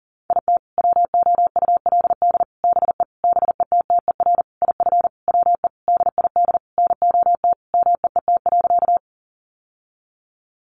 Morse code, heard as a keyed tone, about 40 wpm, 700 Hz. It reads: IT WOULD BE BETTER IF WE DID NOT MEET.
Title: Sentences from Top 1100 Words 40wpm